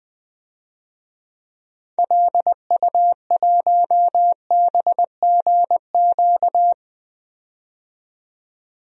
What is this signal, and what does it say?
Morse code, heard as a keyed tone, about 20 wpm, 700 Hz. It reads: LU1BGQ
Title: DXCC-100 Call Signs - 6 Characters - 20wpm